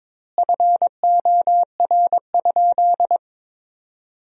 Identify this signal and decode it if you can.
Morse code, heard as a keyed tone, about 22 wpm, 700 Hz. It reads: FOR?
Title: 2x Word-Spacing - Sentences from Top 100 Words - 22wpm